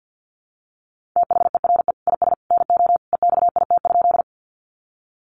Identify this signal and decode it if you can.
Morse code, heard as a keyed tone, about 48 wpm, 700 Hz. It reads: THERE IS NO EXIT?
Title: Sentences from Top 1100 Words 50wpm